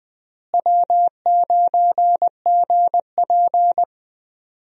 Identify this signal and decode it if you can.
Morse code, heard as a keyed tone, about 20 wpm, 700 Hz. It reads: W9GP
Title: USA Call Signs - 4 Characters - Repeated 2x - 20wpm